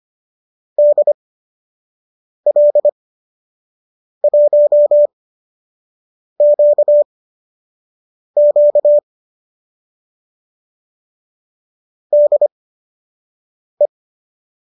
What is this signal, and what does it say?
Morse code, heard as a keyed tone, about 25 wpm, spaced as for 6 wpm, 590 Hz. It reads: DL1QQ DE
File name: qso012 06